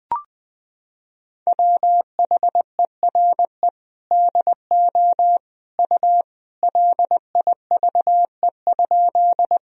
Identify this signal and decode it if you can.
Morse code, heard as a keyed tone, about 21 wpm, 700 Hz. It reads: WHERE DO U LIVE?
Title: Sentences from Top 200 Words and Abbreviations 20wpm